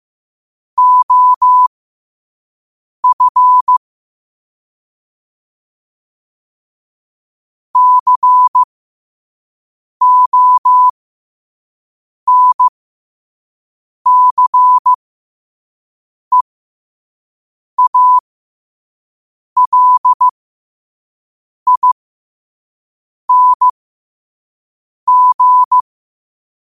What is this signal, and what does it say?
Morse code, heard as a keyed tone, about 15 wpm, spaced as for 5 wpm, 1000 Hz. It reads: OF CONCEALING
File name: qotdfiles / 5wpm